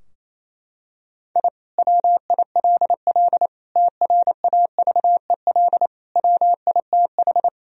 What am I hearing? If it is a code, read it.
Morse code, heard as a keyed tone, about 28 wpm, 700 Hz. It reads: I WILL TRAVEL WITH
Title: Sentences from Top 1000 Words - Review Part 2 - 28wpm